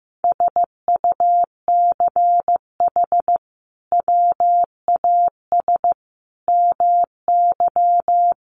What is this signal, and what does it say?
Morse code, heard as a keyed tone, about 15 wpm, 700 Hz. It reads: SUCH WAS MY
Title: Sentences from Top 800 Words 15wpm